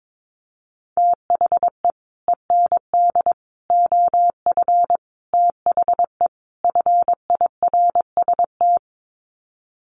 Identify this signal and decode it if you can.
Morse code, heard as a keyed tone, about 22 wpm, 700 Hz. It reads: THE END OF THE FIRST
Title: Sets of 5 Words Encore 22wpm